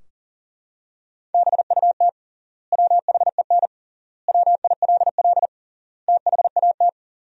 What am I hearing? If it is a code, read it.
Morse code, heard as a keyed tone, about 37 wpm, 700 Hz. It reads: BUT WHEN WILL THAT
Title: 3x Word-Spacing - Sentences from Top 100 Words - 40wpm